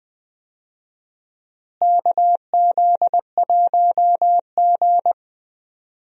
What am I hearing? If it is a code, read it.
Morse code, heard as a keyed tone, about 20 wpm, 700 Hz. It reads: KZ1G
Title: USA Call Signs - 4 Characters - Repeated 3x - 20wpm